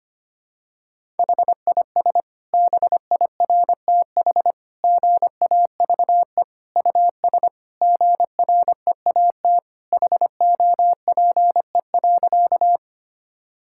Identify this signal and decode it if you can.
Morse code, heard as a keyed tone, about 25 wpm, 700 Hz. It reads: HIS BIRTH GAVE US GREAT HOPE.
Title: Sentences from Top 1000 Words - Review Part 5 - 25wpm